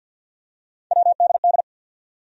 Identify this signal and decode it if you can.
Morse code, heard as a keyed tone, about 53 wpm, 700 Hz. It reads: WDD
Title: USA Call Sign Postfixes - 3 Letters 50wpm